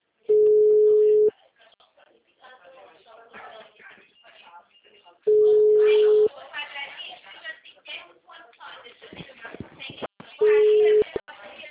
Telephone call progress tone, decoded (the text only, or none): ringback tone